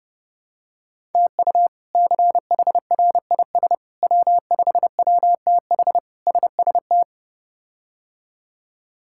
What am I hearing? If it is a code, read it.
Morse code, heard as a keyed tone, about 30 wpm, 700 Hz. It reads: TU CHRIS W5WTH SST
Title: K1USN SST QSO Practice - 30wpm